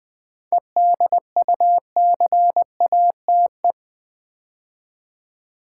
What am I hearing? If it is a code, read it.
Morse code, heard as a keyed tone, about 20 wpm, 700 Hz. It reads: EDUCATE
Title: 7 Letter Words - First 3 Letters Given - 20wpm